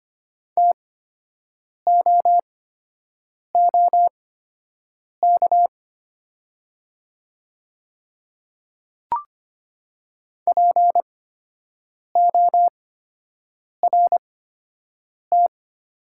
Morse code, heard as a keyed tone, about 25 wpm, spaced as for 7 wpm, 700 Hz. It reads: TOOK PORT
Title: 8x Character Spacing - Top 300 Words - 25wpm